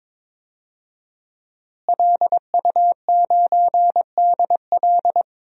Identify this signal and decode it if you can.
Morse code, heard as a keyed tone, about 22 wpm, 700 Hz. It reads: LU9DL
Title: DXCC-100 Call Signs - 5 Characters - Repeated 2x - 22wpm